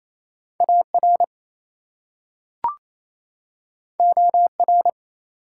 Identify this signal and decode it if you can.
Morse code, heard as a keyed tone, about 28 wpm, 700 Hz. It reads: AR OR
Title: US States and Canadian Provinces 28wpm